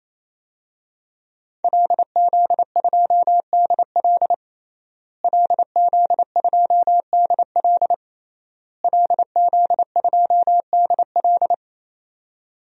Morse code, heard as a keyed tone, about 28 wpm, 700 Hz. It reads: LZ2DL LZ2DL LZ2DL
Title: DXCC-100 Call Signs - 5 Characters - Repeated 3x - 28wpm